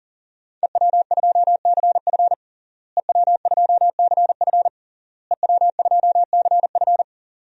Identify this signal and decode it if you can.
Morse code, heard as a keyed tone, about 40 wpm, 700 Hz. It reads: EW2CF EW2CF EW2CF
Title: DXCC-100 Call Signs - 5 Characters - Repeated 3x - 40wpm